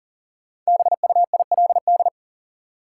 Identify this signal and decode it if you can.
Morse code, heard as a keyed tone, about 40 wpm, 700 Hz. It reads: BUILD
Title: Sentences from Top 1000 Words - Review Part 1 - New Word Drill - 40wpm